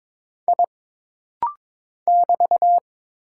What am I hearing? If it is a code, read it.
Morse code, heard as a keyed tone, about 22 wpm, 700 Hz. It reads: I <BT>
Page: This rapid-fire timing is geared to push your ICR (Instant Character Recognition) skills.